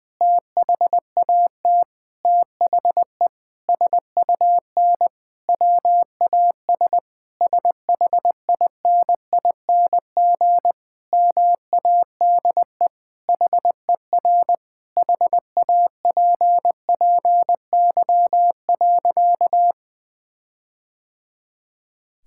Morse code, heard as a keyed tone, about 20 wpm, 700 Hz. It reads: THAT THE SUN WAS SHINING MADE HER HAPPY.
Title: Sentences with 4-Word Phrases - Spoken Beginning - 20wpm